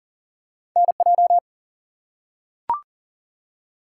Morse code, heard as a keyed tone, about 39 wpm, 700 Hz. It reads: NJ
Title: US States and Canadian Provinces 40wpm